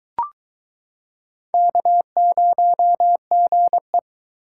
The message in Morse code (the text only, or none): K0GE